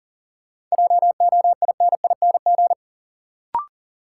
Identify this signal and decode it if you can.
Morse code, heard as a keyed tone, about 40 wpm, 700 Hz. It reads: JOINING
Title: Sets of 5 Words - New Word Drill - 40wpm